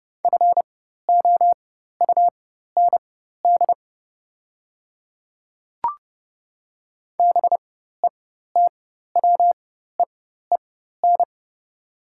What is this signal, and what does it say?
Morse code, heard as a keyed tone, about 30 wpm, spaced as for 14 wpm, 700 Hz. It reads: FOUND BETWEEN
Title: 4x Character Spacing - Top 300 Words - 30wpm